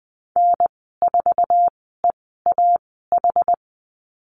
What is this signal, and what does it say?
Morse code, heard as a keyed tone, about 20 wpm, spaced as for 15 wpm, 700 Hz. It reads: N4EAH
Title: 076 Lesson 13g - Letter L - Call Signs 20wpm